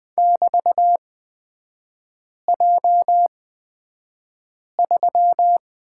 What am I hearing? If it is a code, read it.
Morse code, heard as a keyed tone, about 20 wpm, spaced as for 5 wpm, 690 Hz. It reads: <BT>J3
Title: 229 Lesson 37e - Letter x - Three Characters 20wpm